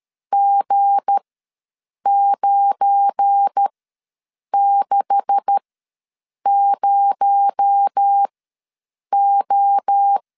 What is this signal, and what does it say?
Morse code, heard as a keyed tone, about 13 wpm, spaced as for 7 wpm, 800 Hz. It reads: G960O
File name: LEZIONI / 9 286 ws 12wpm 3351